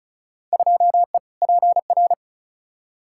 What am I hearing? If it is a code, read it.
Morse code, heard as a keyed tone, about 36 wpm, 700 Hz. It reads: 2E PR
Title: Field Day Exchanges Encore - 35wpm